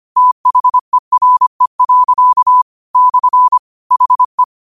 Morse code, heard as a keyed tone, about 25 wpm, 1000 Hz. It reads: THERE. / HE
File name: qotdfiles / 25wpm